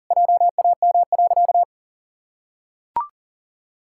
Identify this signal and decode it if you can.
Morse code, heard as a keyed tone, about 40 wpm, 700 Hz. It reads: JAM.
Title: Pangram Challenge v2 - 40wpm